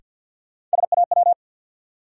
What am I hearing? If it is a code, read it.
Morse code, heard as a keyed tone, about 53 wpm, 700 Hz. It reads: SAW
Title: Sentences from Top 300 Words - New Word Drill - 50wpm